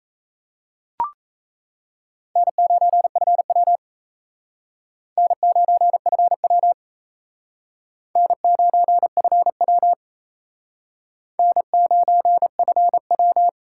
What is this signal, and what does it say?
Morse code, heard as a keyed tone, about 35 wpm, 700 Hz. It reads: N9FW N9FW N9FW N9FW